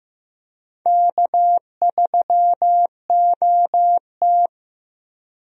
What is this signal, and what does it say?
Morse code, heard as a keyed tone, about 15 wpm, 700 Hz. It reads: K3OT
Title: USA Call Signs - 4 Characters - 15wpm